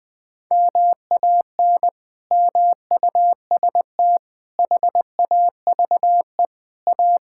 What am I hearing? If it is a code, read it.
Morse code, heard as a keyed tone, about 20 wpm, 700 Hz. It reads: MAN MUST HAVE A
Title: Sentences from Top 1000 Words - Review Part 6 - 20wpm